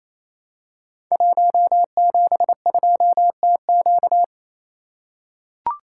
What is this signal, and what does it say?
Morse code, heard as a keyed tone, about 28 wpm, 700 Hz. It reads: 172TQ
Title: Code Groups - 5 Letter:Numbers - 28wpm